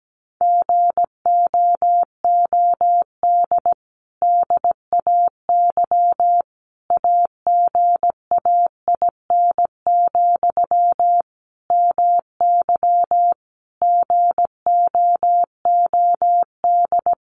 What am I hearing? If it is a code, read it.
Morse code, heard as a keyed tone, about 17 wpm, 700 Hz. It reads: GOOD DAY AGAIN, MY GOOD
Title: Sentences from Top 1100 Words 17wpm